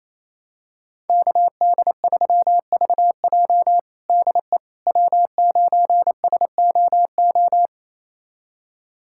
Morse code, heard as a keyed tone, about 27 wpm, 700 Hz. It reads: KD3VJ DE W9SOO
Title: USA Callsign DE Callsign - 5 Characters - 28wpm